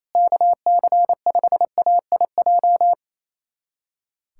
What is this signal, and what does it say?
Morse code, heard as a keyed tone, about 28 wpm, 700 Hz. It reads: KC5AIJ
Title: USA Callsign DE Callsign - 6 Characters - 28wpm